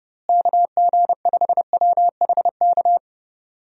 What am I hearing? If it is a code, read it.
Morse code, heard as a keyed tone, about 31 wpm, 700 Hz. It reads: KG5WHK
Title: USA Call Signs - 6 Characters - Repeated 2x - 30wpm